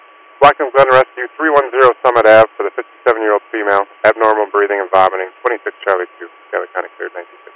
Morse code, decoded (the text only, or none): none